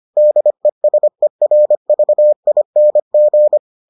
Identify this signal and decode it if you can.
Morse code, heard as a keyed tone, about 25 wpm, 600 Hz. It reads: DESERVING